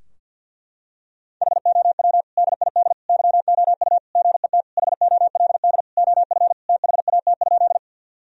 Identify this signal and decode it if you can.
Morse code, heard as a keyed tone, about 50 wpm, 700 Hz. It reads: HOW DID YOU GET HOLD OF THAT?
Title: Sentences from Top 1000 Words - Review Part 2 - 50wpm